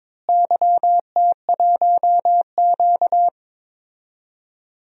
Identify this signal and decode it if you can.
Morse code, heard as a keyed tone, about 22 wpm, 700 Hz. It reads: YT1Q
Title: DXCC-100 Call Signs - 4 Characters - Repeated 3x - 22wpm